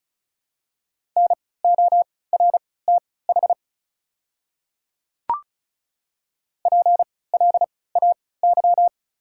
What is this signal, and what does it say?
Morse code, heard as a keyed tone, about 34 wpm, spaced as for 20 wpm, 700 Hz. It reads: NORTH PLAY
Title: 3x Character Spacing - Top 300 Words - 35wpm